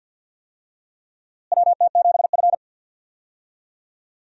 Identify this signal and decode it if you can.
Morse code, heard as a keyed tone, about 50 wpm, 700 Hz. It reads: WT7F
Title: USA Call Signs - 4 Characters - Repeated 3x - 50wpm